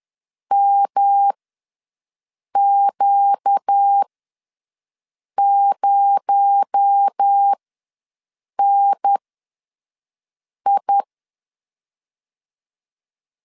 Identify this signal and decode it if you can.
Morse code, heard as a keyed tone, about 10 wpm, spaced as for 5 wpm, 800 Hz. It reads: MQ0NI
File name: LEZIONI / 7 GF9Z / 7 G ws 10wpm 3843